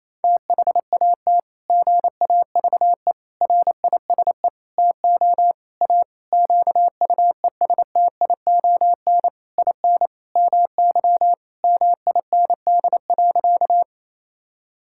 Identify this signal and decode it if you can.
Morse code, heard as a keyed tone, about 28 wpm, 700 Hz. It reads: THAT GAVE RISE TO A QUESTION IN MY MIND.
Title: Sentences from Top 1000 Words - Review Part 6 - 28wpm